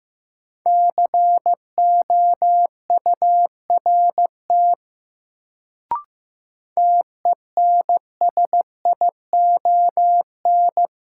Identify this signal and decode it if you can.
Morse code, heard as a keyed tone, about 15 wpm, 700 Hz. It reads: COURT TENSION
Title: Sentences from Top 1600 Words - New Word Drill - 15wpm